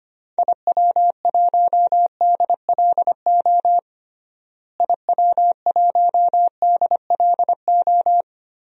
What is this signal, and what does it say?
Morse code, heard as a keyed tone, about 25 wpm, 700 Hz. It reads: IW1DLO IW1DLO
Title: DXCC-100 Call Signs - 6 Characters - Repeated 2x - 25wpm